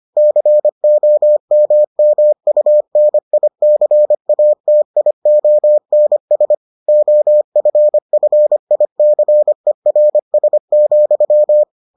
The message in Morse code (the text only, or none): COMMUNICATIONS OFFICERS,